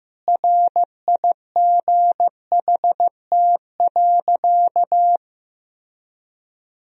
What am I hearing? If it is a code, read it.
Morse code, heard as a keyed tone, about 15 wpm, 700 Hz. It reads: RIGHT.
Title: Sentences from Top 1000 Words - Review Part 4 - 15wpm